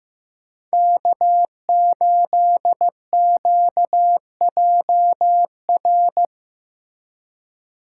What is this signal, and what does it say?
Morse code, heard as a keyed tone, about 15 wpm, 700 Hz. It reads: K8QJR
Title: USA Call Signs - 5 Characters - 3x Character Spacing - 15wpm